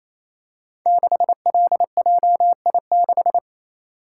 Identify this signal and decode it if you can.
Morse code, heard as a keyed tone, about 29 wpm, 700 Hz. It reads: 6LJI6
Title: Code Groups - 5 Letter:Numbers - 28wpm